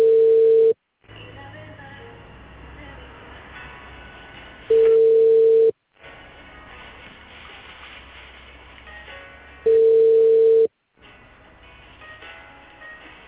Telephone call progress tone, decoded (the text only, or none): ringback tone